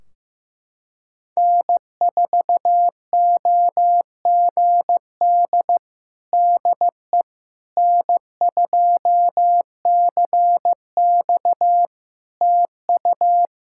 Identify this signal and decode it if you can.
Morse code, heard as a keyed tone, about 15 wpm, 700 Hz. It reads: N4OGD DE N2CX TU
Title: Practice QSO - Exchange 4 - Ending - 15wpm